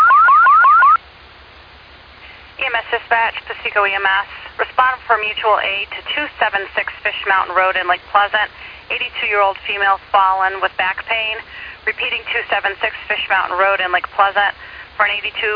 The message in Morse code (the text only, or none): none